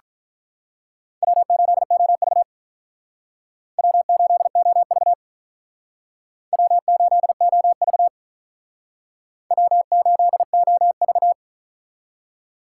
Morse code, heard as a keyed tone, about 44 wpm, 700 Hz. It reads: W8OV W8OV W8OV W8OV